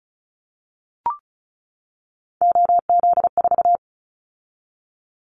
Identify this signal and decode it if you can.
Morse code, heard as a keyed tone, about 36 wpm, 700 Hz. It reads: OZ4
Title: Callsign Prefixes for Top 50 DXCC - 35wpm